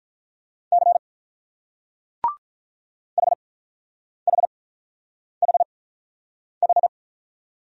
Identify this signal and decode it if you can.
Morse code, heard as a keyed tone, about 44 wpm, spaced as for 9 wpm, 700 Hz. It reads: C HHHH